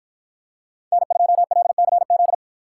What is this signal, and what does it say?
Morse code, heard as a keyed tone, about 53 wpm, 700 Hz. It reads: N2LCZ